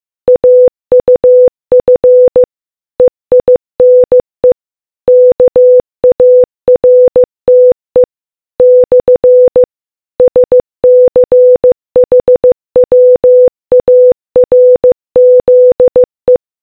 Morse code, heard as a keyed tone, about 15 wpm, 500 Hz. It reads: AUF EINE KARTE / SCHWARZE